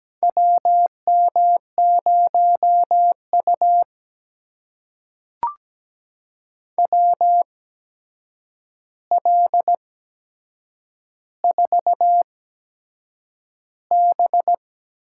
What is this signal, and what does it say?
Morse code, heard as a keyed tone, about 17 wpm, spaced as for 7 wpm, 700 Hz. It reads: WM0U WL4B
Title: USA Call Signs - 4 Characters - 8x Character Spacing - 17wpm